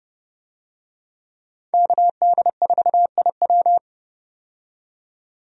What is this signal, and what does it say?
Morse code, heard as a keyed tone, about 30 wpm, 700 Hz. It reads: KD4IW
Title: USA Call Signs - 5 Characters - 30wpm